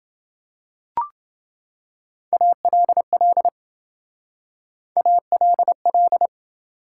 Morse code, heard as a keyed tone, about 27 wpm, 700 Hz. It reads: ALL ALL